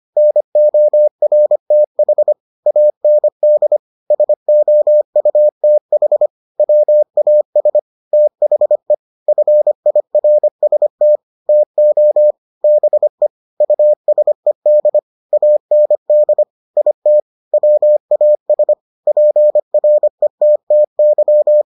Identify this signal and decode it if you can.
Morse code, heard as a keyed tone, about 25 wpm, 600 Hz. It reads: NORTH AND SOUTH WAS THE FIRST TO BE USED AND IT WAS PRETTY